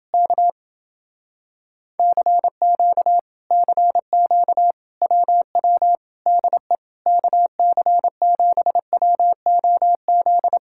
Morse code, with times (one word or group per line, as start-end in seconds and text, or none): 0.1-0.5 K
2.0-3.2 CQ
3.5-4.7 CQ
5.0-6.0 WW
6.3-6.8 DE
7.1-10.6 KC7WOZ